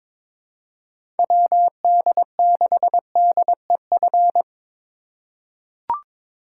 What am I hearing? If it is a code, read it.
Morse code, heard as a keyed tone, about 21 wpm, 700 Hz. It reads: WD6DEF
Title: USA Call Signs - 6 Characters - 3x Character Spacing - 22wpm